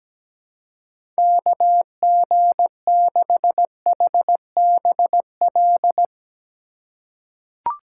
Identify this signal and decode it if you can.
Morse code, heard as a keyed tone, about 17 wpm, 700 Hz. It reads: KG6HBL
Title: USA Call Signs - 2x Character Spacing - 17wpm